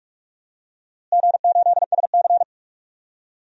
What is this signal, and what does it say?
Morse code, heard as a keyed tone, about 46 wpm, 700 Hz. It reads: G8SC
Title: DXCC-100 Call Signs - 4 Characters - Repeated 3x - 45wpm